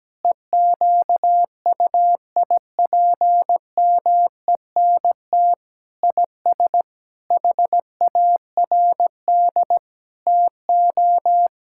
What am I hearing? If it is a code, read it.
Morse code, heard as a keyed tone, about 17 wpm, 700 Hz. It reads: EQUIPMENT IS HARD TO
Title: Sentences from Top 1000 Words - Review Part 5 - 17wpm